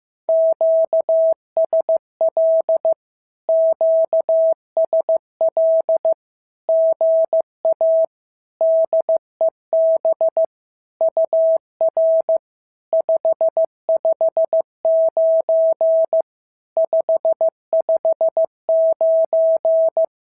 Morse code, heard as a keyed tone, about 15 wpm, 650 Hz. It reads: QSL QSL GA DEB UR 559 559
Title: SOTA QSO Practice 15wpm